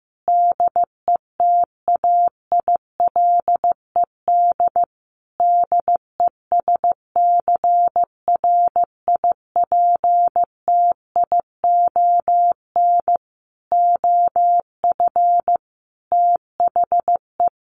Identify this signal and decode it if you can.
Morse code, heard as a keyed tone, about 15 wpm, 700 Hz. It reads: DETAILED DESCRIPTION OF THE
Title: Sentences from Top 1500 Words 15wpm